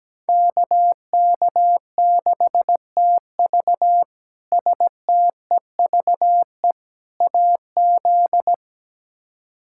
Morse code, heard as a keyed tone, about 17 wpm, 700 Hz. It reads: KK6TV STEVE AZ
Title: K1USN SST QSO Practice - 17wpm